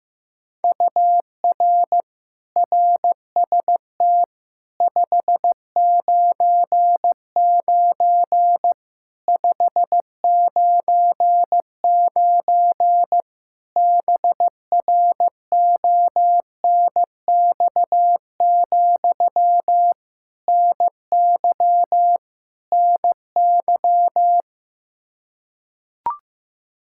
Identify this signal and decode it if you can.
Morse code, heard as a keyed tone, about 15 wpm, 700 Hz. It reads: UR RST 599 599 BRONX, NY NY
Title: Practice QSO - Exchange 1 - Beginning - 15wpm